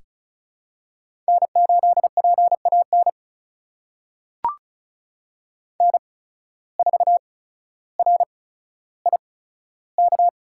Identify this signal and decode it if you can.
Morse code, heard as a keyed tone, about 34 wpm, spaced as for 15 wpm, 700 Hz. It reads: N8PAN N4RIK
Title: USA Call Signs - 5 Characters - 8x Character Spacing - 35wpm